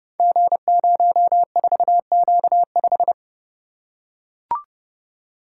Morse code, heard as a keyed tone, about 29 wpm, 700 Hz. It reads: G04Q5
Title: Code Groups - 5 Letter:Numbers - 30wpm